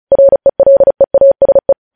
Morse code, heard as a keyed tone, about 36 wpm, 550 Hz. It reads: RELEASE
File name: podcast / News headlines-1x-35x35